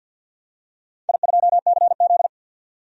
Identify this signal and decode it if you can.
Morse code, heard as a keyed tone, about 52 wpm, 700 Hz. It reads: I2CZ
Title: DXCC-100 Call Signs - 4 Characters - 50wpm